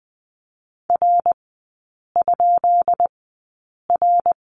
Morse code, heard as a keyed tone, about 20 wpm, spaced as for 8 wpm, 690 Hz. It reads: R?R
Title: 133 Lesson 22e - Question Mark - Three Characters 20wpm